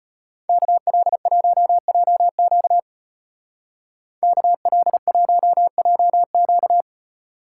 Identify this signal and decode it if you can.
Morse code, heard as a keyed tone, about 31 wpm, 700 Hz. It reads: KL1JQ KL1JQ
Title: Speed Racing - USA 2x2 Call Signs 25wpm